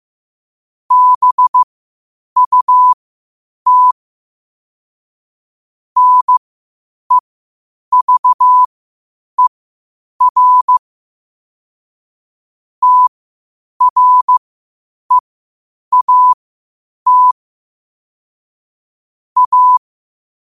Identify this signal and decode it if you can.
Morse code, heard as a keyed tone, about 15 wpm, spaced as for 8 wpm, 1000 Hz. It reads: BUT NEVER TREAT A
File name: qotdfiles / 8wpm